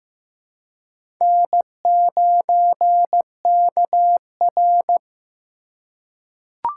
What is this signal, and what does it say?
Morse code, heard as a keyed tone, about 15 wpm, 700 Hz. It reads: N9KR
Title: K1USN SST QSO Practice - 15wpm